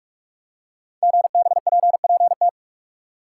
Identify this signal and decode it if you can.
Morse code, heard as a keyed tone, about 44 wpm, 690 Hz. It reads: GBPPT
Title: Code Groups - 5 Letters - 45wpm